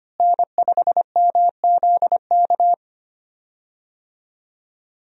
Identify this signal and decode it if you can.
Morse code, heard as a keyed tone, about 25 wpm, 700 Hz. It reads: N5MZK
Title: USA Call Signs - 5 Characters - Repeated 2x - 25wpm